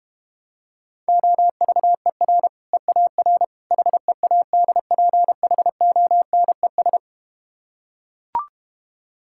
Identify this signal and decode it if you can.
Morse code, heard as a keyed tone, about 32 wpm, 700 Hz. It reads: OVER EAR HEADPHONES